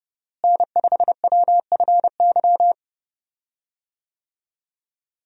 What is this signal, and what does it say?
Morse code, heard as a keyed tone, about 30 wpm, 700 Hz. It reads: N5WFY
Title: USA Call Signs - 5 Characters - Repeated 3x - 30wpm